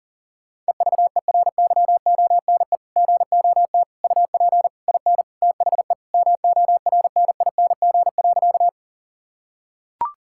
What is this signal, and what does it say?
Morse code, heard as a keyed tone, about 40 wpm, 700 Hz. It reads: EVERYONE GOT UP IN THE MORNING.
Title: Sentences from Top 1000 Words - Review Part 5 - 40wpm